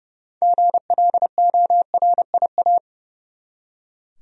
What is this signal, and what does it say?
Morse code, heard as a keyed tone, about 30 wpm, 700 Hz. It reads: GLORIA
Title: USA Common Names - 30wpm